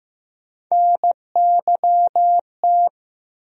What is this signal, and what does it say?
Morse code, heard as a keyed tone, about 15 wpm, 700 Hz. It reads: NYT
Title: USA Call Sign Postfixes - 3 Letters 15wpm